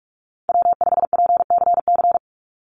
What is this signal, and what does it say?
Morse code, heard as a keyed tone, about 44 wpm, 700 Hz. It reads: W5PCC
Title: USA Call Signs 45wpm